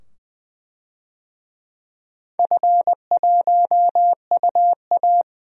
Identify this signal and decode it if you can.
Morse code, heard as a keyed tone, about 21 wpm, 700 Hz. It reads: F1UA
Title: DXCC-100 Call Signs - 4 Characters - 20wpm